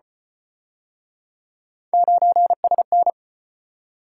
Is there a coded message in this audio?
Yes: Morse code from 1.9 to 3.1 s, heard as a keyed tone, about 31 wpm, 700 Hz. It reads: 9SN